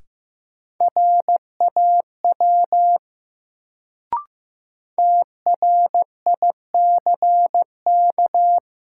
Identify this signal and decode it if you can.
Morse code, heard as a keyed tone, about 15 wpm, 700 Hz. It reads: RAW TRICK
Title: Sentences from Top 1500 Words - New Word Drill Alt - 15wpm